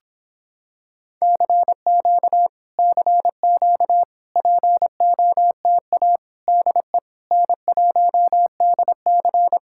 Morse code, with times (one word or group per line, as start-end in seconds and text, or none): 1.2-2.5 CQ
2.8-4.0 CQ
4.4-6.2 POTA
6.5-7.0 DE
7.3-9.6 N1DC